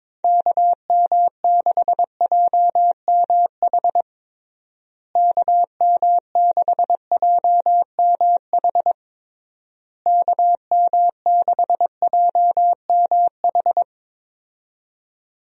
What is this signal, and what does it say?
Morse code, heard as a keyed tone, about 22 wpm, 700 Hz. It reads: KM6JMH KM6JMH KM6JMH
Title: USA Call Signs - 6 Characters - Repeated 3x - 22wpm